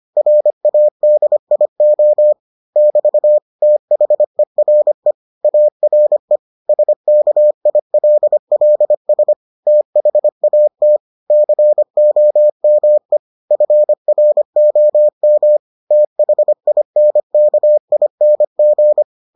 Morse code, heard as a keyed tone, about 25 wpm, 600 Hz. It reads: RADIO <BT> THERE ARE SKILLS THAT COME FROM THINKING